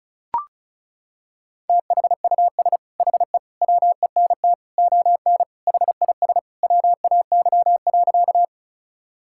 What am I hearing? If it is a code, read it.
Morse code, heard as a keyed tone, about 36 wpm, 700 Hz. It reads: THUS HE WENT ON HIS WAY.
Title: Sentences from Top 1000 Words - Review Part 5 - 35wpm